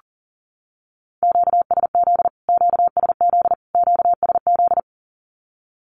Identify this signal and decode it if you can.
Morse code, heard as a keyed tone, about 40 wpm, 700 Hz. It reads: QSZ QSZ QSZ
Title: Q-Codes Repeated 3x 40wpm